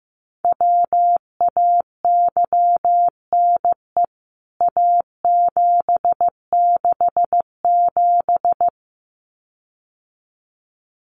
Morse code, heard as a keyed tone, about 15 wpm, 700 Hz. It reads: WAYNE A767
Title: CWT Exchange 15wpm